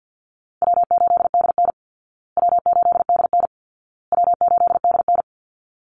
Morse code, heard as a keyed tone, about 48 wpm, 700 Hz. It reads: W8DN W8DN W8DN
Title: Easy Call Signs Repeated 3x - 50wpm